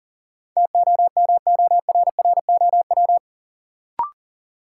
Morse code, heard as a keyed tone, about 40 wpm, 700 Hz. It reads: TOMORROW
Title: Sentences from Top 1100 Words - New Word Drill Alt - 40wpm